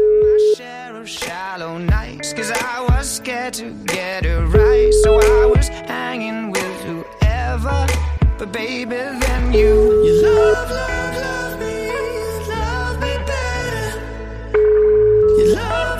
Ringback tone (Europe, 425 Hz), with noise.